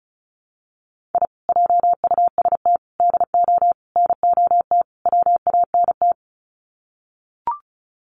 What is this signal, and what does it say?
Morse code, heard as a keyed tone, about 34 wpm, 700 Hz. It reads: I JUST DO NOT WANT
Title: Sets of 5 Words Encore2 35wpm